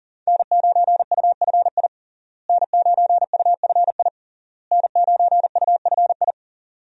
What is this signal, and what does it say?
Morse code, heard as a keyed tone, about 40 wpm, 700 Hz. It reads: N9UFI N9UFI N9UFI
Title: USA Call Signs - 5 Characters - Repeated 3x - 40wpm